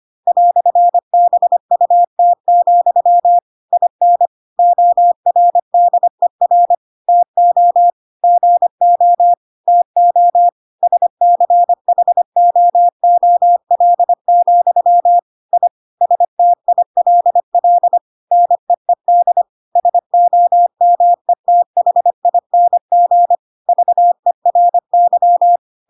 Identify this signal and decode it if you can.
Morse code, heard as a keyed tone, about 25 wpm, 700 Hz. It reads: "BUT, IN ORDER TO GO TO SCHOOL, I STILL NEED SOMETHING VERY
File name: Pinocchio / Pinocchio ch8 25 25 ik1hge